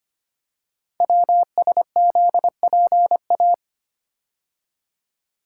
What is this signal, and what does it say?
Morse code, heard as a keyed tone, about 25 wpm, 700 Hz. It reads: WSZPA
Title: Code Groups - 5 Letters - 25wpm